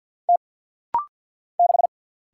Morse code, heard as a keyed tone, about 50 wpm, 700 Hz. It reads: T 6
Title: Single Letter-Number - Rapid-Fire 50wpm